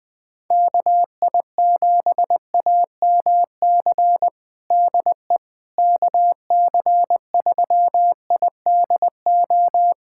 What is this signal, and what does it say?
Morse code, heard as a keyed tone, about 20 wpm, 700 Hz. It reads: KI7AMC DE KC3IDO
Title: USA Callsign DE Callsign 20wpm